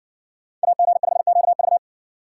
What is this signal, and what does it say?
Morse code, heard as a keyed tone, about 58 wpm, 700 Hz. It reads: AG5QV